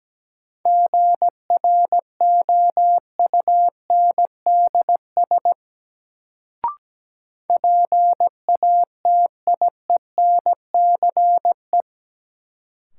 Morse code, heard as a keyed tone, about 17 wpm, 700 Hz. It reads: GROUNDS PATIENCE
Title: Sentences from Top 1300 Words - New Word Drill - 17wpm